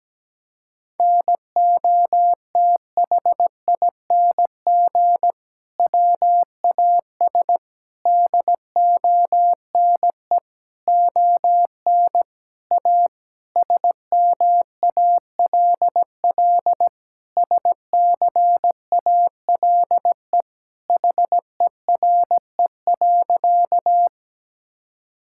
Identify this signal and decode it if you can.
Morse code, heard as a keyed tone, about 17 wpm, 700 Hz. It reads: NOTHING WAS DONE ON A SMALL SCALE HERE.
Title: Sentences from Top 1000 Words - Review Part 5 - 17wpm